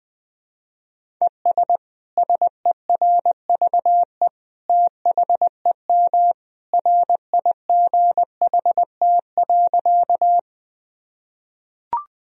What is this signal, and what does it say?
Morse code, heard as a keyed tone, about 21 wpm, 700 Hz. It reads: ES SERVE THEM RIGHT.
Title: Sentences from Top 400 Words and Abbreviations 20wpm